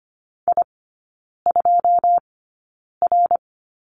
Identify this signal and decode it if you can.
Morse code, heard as a keyed tone, about 25 wpm, spaced as for 9 wpm, 690 Hz. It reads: I2R
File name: Lesson 14e - Number 2 - Three Characters 25wpm